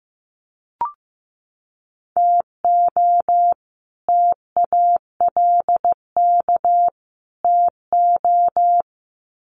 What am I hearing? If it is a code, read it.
Morse code, heard as a keyed tone, about 15 wpm, 700 Hz. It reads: TO TALK TO
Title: Sets of 3 Words 15wpm